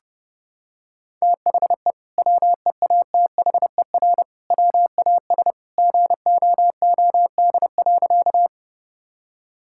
Morse code, heard as a keyed tone, about 30 wpm, 700 Hz. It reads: THE WEATHER WAS GOOD.
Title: Sentences from Top 1000 Words - Review Part 5 - 30wpm